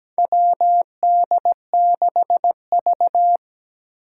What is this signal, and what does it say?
Morse code, heard as a keyed tone, about 17 wpm, 700 Hz. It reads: WD6V
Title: Four Challenging Letter-Number Combinations 17wpm